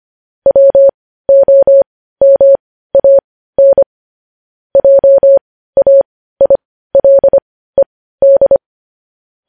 Morse code, heard as a keyed tone, about 25 wpm, spaced as for 15 wpm, 550 Hz. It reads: WOMAN JAILED
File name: podcast / News headlines-1x-25x15